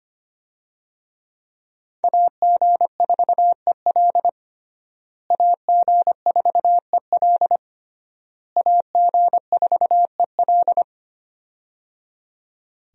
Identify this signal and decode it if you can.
Morse code, heard as a keyed tone, about 25 wpm, 700 Hz. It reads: AG4EL AG4EL AG4EL
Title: USA Call Signs - 5 Characters - Repeated 3x - 25wpm